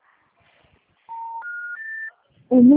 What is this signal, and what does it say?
Special information tone.